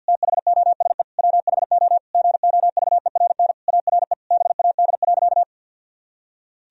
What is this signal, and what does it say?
Morse code, heard as a keyed tone, about 52 wpm, 700 Hz. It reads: THOSE WHO GOVERN ARE BAD.
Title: Sentences from Top 1000 Words - Review Part 2 - 50wpm